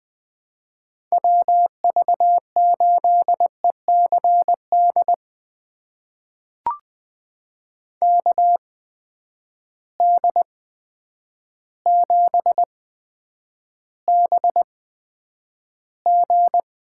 Morse code, heard as a keyed tone, about 20 wpm, spaced as for 9 wpm, 700 Hz. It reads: WV8ECD KD7BG
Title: USA Call Signs - 6 Characters - 8x Character Spacing - 20wpm